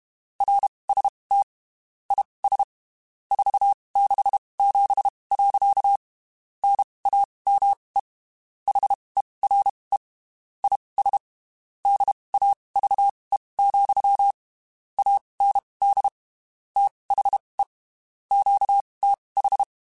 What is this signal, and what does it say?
Morse code, heard as a keyed tone, about 32 wpm, spaced as for 23 wpm, 800 Hz. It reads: RST IS 467. NAME HERE IS DAVE, AND THE QTH
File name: (spaced) / QSOs Spaced-mp3s / kc7sac @30 sp